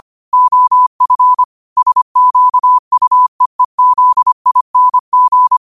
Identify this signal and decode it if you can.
Morse code, heard as a keyed tone, about 25 wpm, 1000 Hz. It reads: OF SQUEEZING